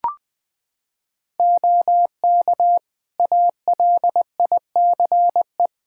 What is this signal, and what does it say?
Morse code, heard as a keyed tone, about 20 wpm, 700 Hz. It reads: OK ALICE
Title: Practice QSO - Exchange 3 - More Info - 20wpm